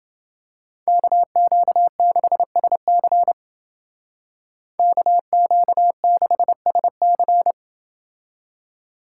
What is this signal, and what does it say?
Morse code, heard as a keyed tone, about 29 wpm, 700 Hz. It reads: KQ6SC KQ6SC